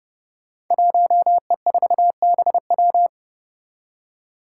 Morse code, heard as a keyed tone, about 30 wpm, 700 Hz. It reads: 1E4BW
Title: Five Challenging Letter-Number Combinations 30wpm